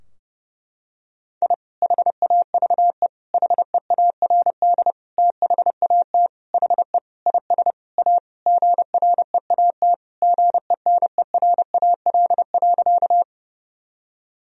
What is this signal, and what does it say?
Morse code, heard as a keyed tone, about 31 wpm, 700 Hz. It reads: I HAVE HEARD THAT HE IS A GREAT GENERAL.
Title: Sentences from Top 1000 Words - Review Part 3 - 30wpm